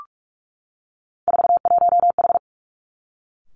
Morse code, heard as a keyed tone, about 46 wpm, 700 Hz. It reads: 41H
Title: Three Challenging Letter-Number Combinations 45wpm